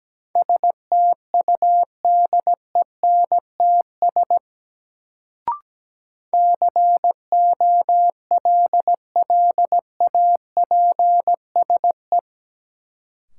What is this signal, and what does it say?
Morse code, heard as a keyed tone, about 17 wpm, 700 Hz. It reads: STUDENTS COLLAPSE
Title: Sets of 5 Words Encore - New Word Drill - 17wpm